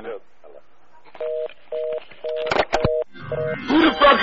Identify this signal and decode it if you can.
Telephone call progress tone (North America), heard as a telephone network's tone, about 480, 620 Hz, with speech. Reorder tone.